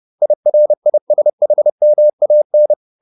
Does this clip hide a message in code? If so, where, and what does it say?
Yes: Morse code from 0.2 to 2.7 s, heard as a keyed tone, about 30 wpm, 600 Hz. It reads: IRISHMAN